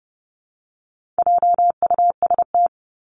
Morse code, heard as a keyed tone, about 30 wpm, 700 Hz. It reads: JUST
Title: Top 300 Words Repeated 30wpm